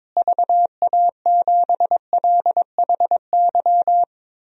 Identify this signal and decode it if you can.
Morse code, heard as a keyed tone, about 22 wpm, 700 Hz. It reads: VA7LHY
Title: DXCC-100 Call Signs - 6 Characters - Repeated 3x - 22wpm